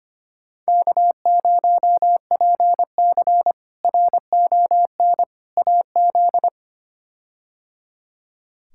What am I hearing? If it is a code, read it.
Morse code, heard as a keyed tone, about 25 wpm, 700 Hz. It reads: K0PC RON AZ
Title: K1USN SST QSO Practice - 25wpm